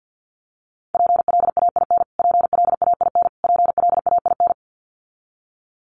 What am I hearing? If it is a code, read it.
Morse code, heard as a keyed tone, about 50 wpm, 700 Hz. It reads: PLAIN PLAIN PLAIN
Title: Top 500 Words Repeated 50wpm